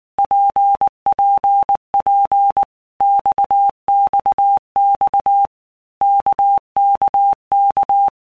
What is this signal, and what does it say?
Morse code, heard as a keyed tone, about 19 wpm, 800 Hz. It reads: PPP XXX KKK
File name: All Groups 1-minute Blocks / Groups@18 / RKPX